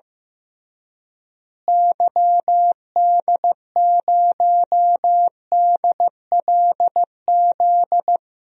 Morse code, heard as a keyed tone, about 15 wpm, 700 Hz. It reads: YD0DLZ
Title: DXCC-100 Call Signs - 6 Characters - Repeated 3x - 15wpm